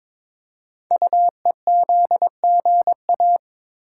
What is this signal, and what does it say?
Morse code, heard as a keyed tone, about 22 wpm, 700 Hz. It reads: UEZGA
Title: Code Groups - 5 Letter:Numbers - 22wpm